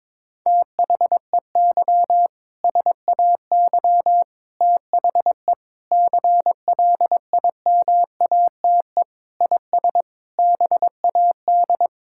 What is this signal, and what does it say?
Morse code, heard as a keyed tone, about 22 wpm, 700 Hz. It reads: THEY SAY THE CLIMATE IS BAD
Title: Sentences from Top 1600 Words 22wpm